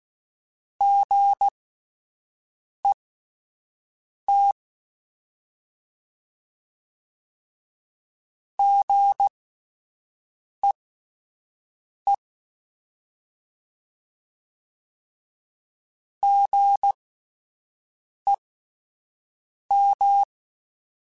Morse code, heard as a keyed tone, about 16 wpm, spaced as for 5 wpm, 800 Hz. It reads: GET GEE GEM
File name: Alphabetical / Alphabetical mp3s / Gs @5